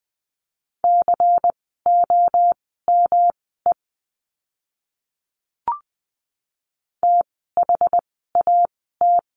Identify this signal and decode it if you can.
Morse code, heard as a keyed tone, about 20 wpm, spaced as for 15 wpm, 700 Hz. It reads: COME THAT
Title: 146 Lesson 24f - Letter y - Words 20wpm